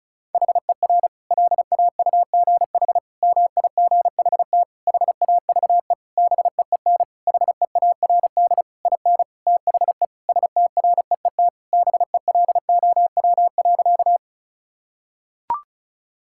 Morse code, heard as a keyed tone, about 36 wpm, 700 Hz. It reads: HER LAUGH MIGHT HAVE BEEN HEARD IN THE STREET BELOW.
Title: Sentences from Top 1000 Words - Review Part 5 - 35wpm